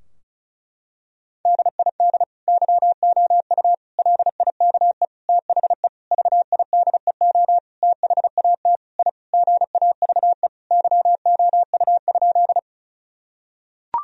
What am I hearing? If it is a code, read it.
Morse code, heard as a keyed tone, about 34 wpm, 700 Hz. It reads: DID YOU LIKE THE VIDEO THAT I GAVE YOU?
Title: Sentences from Top 1000 Words - Review Part 4 - 35wpm